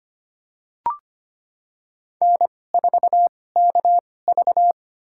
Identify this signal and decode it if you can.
Morse code, heard as a keyed tone, about 25 wpm, spaced as for 18 wpm, 700 Hz. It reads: N4KV
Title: USA Call Signs - 4 Characters - 2x Character Spacing - 25wpm